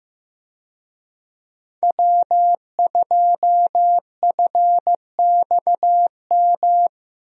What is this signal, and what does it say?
Morse code, heard as a keyed tone, about 15 wpm, 700 Hz. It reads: W2FXM
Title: USA Call Signs - 5 Characters - Repeated 2x - 15wpm